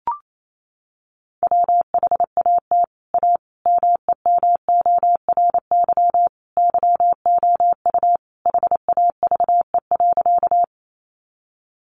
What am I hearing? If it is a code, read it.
Morse code, heard as a keyed tone, about 28 wpm, 700 Hz. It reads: WHAT A MEMORY YOU HAVE.
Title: Sentences from Top 1300 Words 28wpm